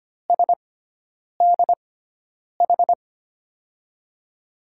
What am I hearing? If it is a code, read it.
Morse code, heard as a keyed tone, about 25 wpm, spaced as for 9 wpm, 700 Hz. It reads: SDH
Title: USA Call Signs - 6 Characters - 6x Character Spacing - 25wpm